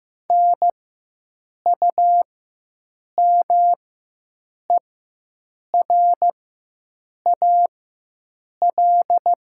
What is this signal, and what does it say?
Morse code, heard as a keyed tone, about 15 wpm, spaced as for 7 wpm, 700 Hz. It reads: NUMERAL
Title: 4x Character Spacing - Top 500 Words - 15wpm